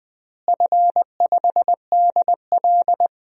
Morse code, heard as a keyed tone, about 21 wpm, 700 Hz. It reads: F5DL
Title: DXCC-100 Call Signs - 4 Characters - Repeated 3x - 20wpm